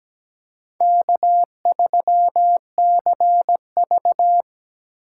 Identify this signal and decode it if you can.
Morse code, heard as a keyed tone, about 17 wpm, 700 Hz. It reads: K3CV
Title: USA Call Signs - 4 Characters - 3x Character Spacing - 17wpm